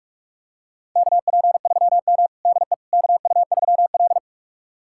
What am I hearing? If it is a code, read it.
Morse code, heard as a keyed tone, about 46 wpm, 700 Hz. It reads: KP3M DE KU3L
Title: USA Callsign DE Callsign - 4 Characters - 45wpm